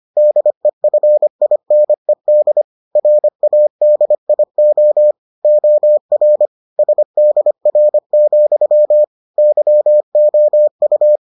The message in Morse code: DEFINED RADIO OR SDR, YOU

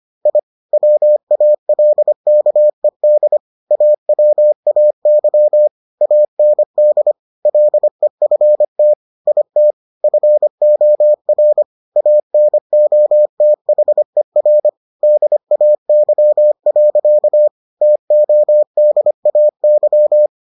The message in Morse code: I WALKED AWAY AND LEFT IT FOR ANOTHER DAY. TODAY

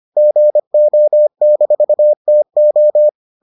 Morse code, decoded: GO-TO

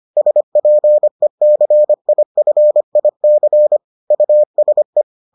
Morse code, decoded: SPECIFIC USE

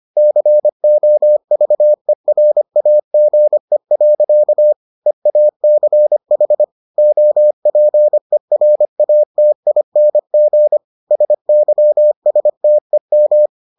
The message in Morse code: COVERAGE. EACH OPERATING SYSTEM

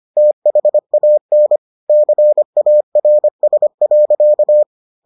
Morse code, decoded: THAN CARS.

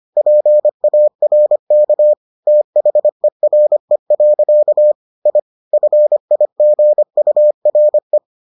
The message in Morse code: PARK THERE. I FIGURE

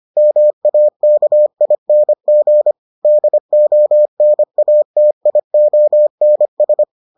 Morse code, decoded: MAKING DONATIONS

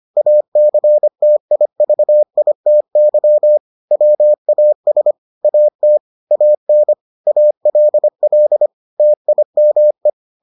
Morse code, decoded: ACTIVITY WAS AT AN ALL TIME

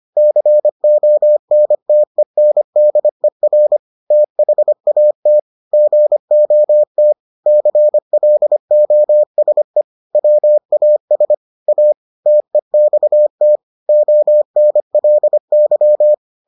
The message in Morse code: CONTENDER THAT GOT CLOSE WAS A TEXT ONLY

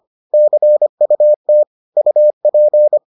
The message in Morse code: CUT UP

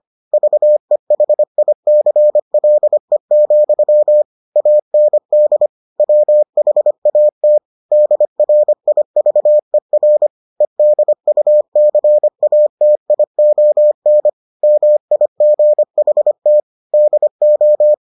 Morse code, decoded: VEHICLE, AND WHAT DRIVER EDUCATION MIGHT DO

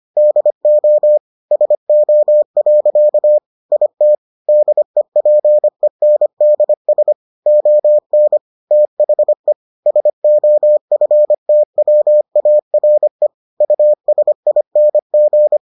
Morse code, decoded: DO SO. IT DEPENDS ON THE SOFTWARE USING

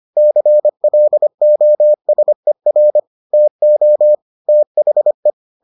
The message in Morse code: CLOSER TO THE

